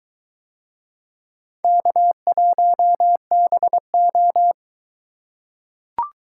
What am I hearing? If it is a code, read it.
Morse code, heard as a keyed tone, about 23 wpm, 700 Hz. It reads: K1BO